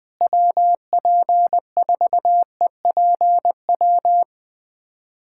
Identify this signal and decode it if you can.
Morse code, heard as a keyed tone, about 21 wpm, 700 Hz. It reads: WP4EPW
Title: USA Callsign DE Callsign - 6 Characters - 20wpm